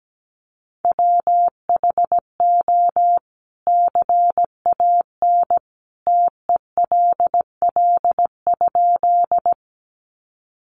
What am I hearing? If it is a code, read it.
Morse code, heard as a keyed tone, about 17 wpm, 700 Hz. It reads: WHO CAN TELL?
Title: A Sound Of Thunder 17wpm